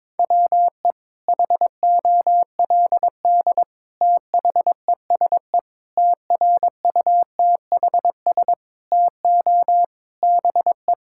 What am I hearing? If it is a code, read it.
Morse code, heard as a keyed tone, about 22 wpm, 700 Hz. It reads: WE HOLD THESE TRUTHS TO BE
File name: Sentences with 9-Word Phrases - 22wpm